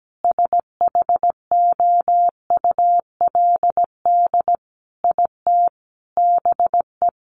Morse code, heard as a keyed tone, about 17 wpm, 700 Hz. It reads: SHOULD IT BE
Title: Sentences from Top 1200 Words 17wpm